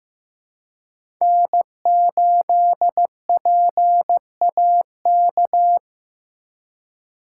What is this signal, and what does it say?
Morse code, heard as a keyed tone, about 15 wpm, 700 Hz. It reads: N8PAK
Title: USA Call Signs - 5 Characters - 2x Character Spacing - 15wpm